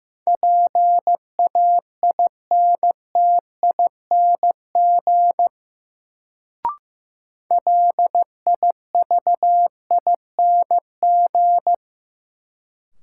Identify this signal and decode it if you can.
Morse code, heard as a keyed tone, about 15 wpm, 700 Hz. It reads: PAINTING LIVING
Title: Sentences from Top 1000 Words - Review Part 6 - New Word Drill - 15wpm